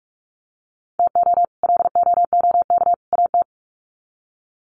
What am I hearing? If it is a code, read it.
Morse code, heard as a keyed tone, about 44 wpm, 700 Hz. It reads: TO LOOK AT
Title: Sets of 3 Words 45wpm